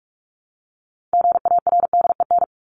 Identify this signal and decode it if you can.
Morse code, heard as a keyed tone, about 44 wpm, 700 Hz. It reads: GARDEN
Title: Top 501-750 Words 45wpm